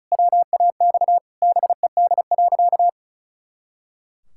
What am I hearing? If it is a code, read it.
Morse code, heard as a keyed tone, about 36 wpm, 700 Hz. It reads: WAX BED.
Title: Pangram Challenge v2 - 35wpm